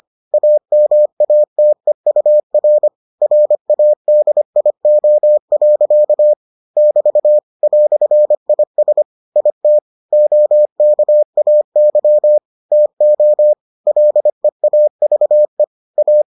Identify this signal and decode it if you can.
Morse code, heard as a keyed tone, about 25 wpm, 600 Hz. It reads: AMATEUR RADIO. <BT> "IS IT OKAY TO LEAVE A